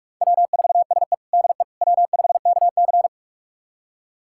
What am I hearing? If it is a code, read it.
Morse code, heard as a keyed tone, about 43 wpm, 700 Hz. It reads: W4SE DE W5KC
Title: USA Callsign DE Callsign - 4 Characters - 45wpm